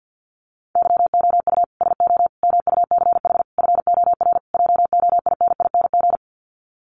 Morse code, heard as a keyed tone, about 48 wpm, 700 Hz. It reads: YOU SO MUCH FOR JOINING
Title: Sets of 5 Words Encore2 50wpm